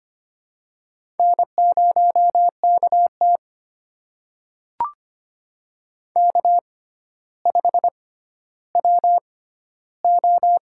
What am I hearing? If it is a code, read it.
Morse code, heard as a keyed tone, about 25 wpm, spaced as for 13 wpm, 700 Hz. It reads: N0KT K5WO
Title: USA Call Signs - 4 Characters - 6x Character Spacing - 25wpm